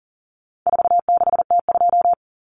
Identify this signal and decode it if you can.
Morse code, heard as a keyed tone, about 40 wpm, 700 Hz. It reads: 46T2
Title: Common Cut Numbers - 4-Digit 40wpm